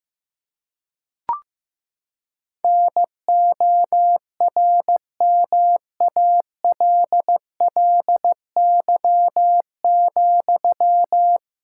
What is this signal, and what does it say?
Morse code, heard as a keyed tone, about 15 wpm, 700 Hz. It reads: NORMALLY,
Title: Sentences from Top 1000 Words - Review Part 6 - 15wpm